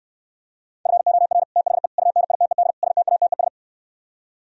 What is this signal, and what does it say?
Morse code, heard as a keyed tone, about 67 wpm, 700 Hz. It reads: FOR THE UNITED STATES